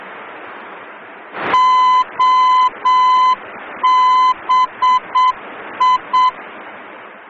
Morse code, heard as a keyed tone, about 7 wpm, 1010 Hz. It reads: OBI
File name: germany / mx / OBI-D